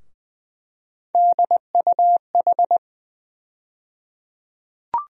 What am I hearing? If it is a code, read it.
Morse code, heard as a keyed tone, about 20 wpm, 700 Hz. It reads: DUH
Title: USA Call Sign Postfixes - 3 Letters 20wpm